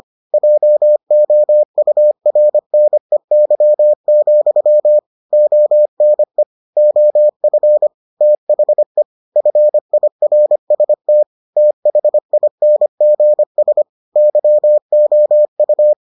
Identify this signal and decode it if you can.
Morse code, heard as a keyed tone, about 25 wpm, 600 Hz. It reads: JOURNEY, ONE OF THE FIRST THINGS YOU